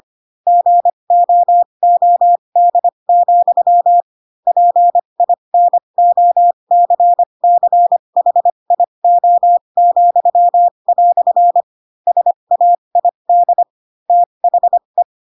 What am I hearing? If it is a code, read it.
Morse code, heard as a keyed tone, about 25 wpm, spaced as for 22 wpm, 700 Hz. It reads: GOOD, PINOCCHIO," SAID THE
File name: Pinocchio / Pinocchio ch4 18 25 ik1hge